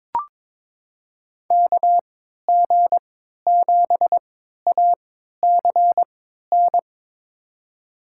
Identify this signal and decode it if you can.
Morse code, heard as a keyed tone, about 22 wpm, spaced as for 13 wpm, 700 Hz. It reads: KG7ACN
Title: USA Call Signs - 6 Characters - 3x Character Spacing - 22wpm